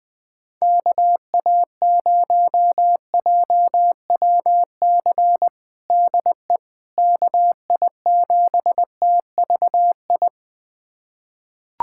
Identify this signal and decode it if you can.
Morse code, heard as a keyed tone, about 20 wpm, 700 Hz. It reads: KA0JWC DE KI7TVI
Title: USA Callsign DE Callsign - 6 Characters - 20wpm